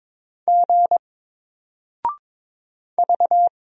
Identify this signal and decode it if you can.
Morse code, heard as a keyed tone, about 22 wpm, 700 Hz. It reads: G V
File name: Single-Letter-Number-22wpm